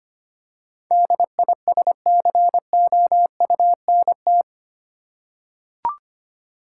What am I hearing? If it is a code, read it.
Morse code, heard as a keyed tone, about 25 wpm, 700 Hz. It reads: DISCOUNT
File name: Eight Letter Words - 25wpm